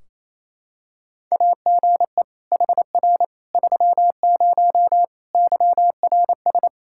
Morse code, heard as a keyed tone, about 28 wpm, 700 Hz. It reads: AGE HR 30 YRS
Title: Practice QSO - Exchange 3 - More Info - 28wpm